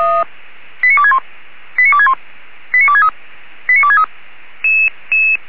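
Five-tone selective call (ZVEI), heard as quick runs of five tones, tones of about 70 ms. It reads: 89261 89261 89263 89263
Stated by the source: ZVEI five-tone call 89284 received by SDR on channel 464.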